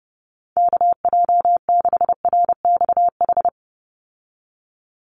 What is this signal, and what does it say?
Morse code, heard as a keyed tone, about 30 wpm, 700 Hz. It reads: KJ6RXH
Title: USA Call Signs 30wpm